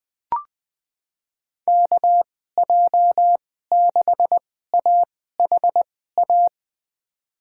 Morse code, heard as a keyed tone, about 20 wpm, spaced as for 14 wpm, 700 Hz. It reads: KJ6AHA
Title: USA Call Signs - 2x Character Spacing - 20wpm